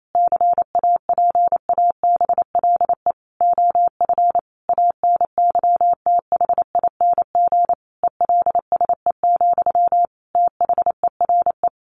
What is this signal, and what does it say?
Morse code, heard as a keyed tone, about 28 wpm, 700 Hz. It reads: CAPABLE OF ANYTHING ELSE, THERE
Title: Sentences from Top 1200 Words 28wpm